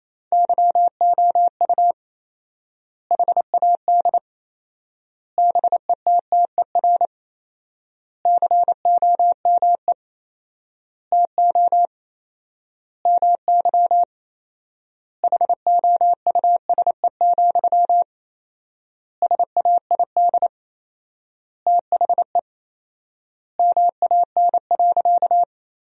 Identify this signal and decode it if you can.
Morse code, heard as a keyed tone, about 28 wpm, 700 Hz. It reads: YOU HAD BETTER COME TO MY HOUSE, SAID THE MAN.
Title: 4x Word-Spacing - Sentences from Top 400 Words - 28wpm